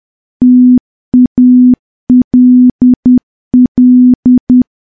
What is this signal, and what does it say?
Morse code, heard as a keyed tone, about 10 wpm, 250 Hz. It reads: TALL
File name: morse / morse-training-10WPM